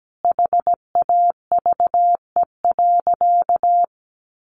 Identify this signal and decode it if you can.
Morse code, heard as a keyed tone, about 17 wpm, 700 Hz. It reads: HAVE.
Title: Sentences from Top 1200 Words 17wpm